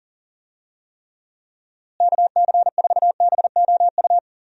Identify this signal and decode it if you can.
Morse code, heard as a keyed tone, about 39 wpm, 700 Hz. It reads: KC4BOU
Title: USA Call Signs - 6 Characters - Repeated 2x - 40wpm